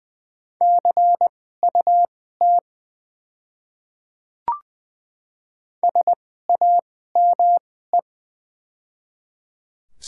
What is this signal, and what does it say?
Morse code, heard as a keyed tone, about 20 wpm, spaced as for 14 wpm, 700 Hz. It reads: CUT SAME
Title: 2x Character Spacing - Top 300 Words - 20wpm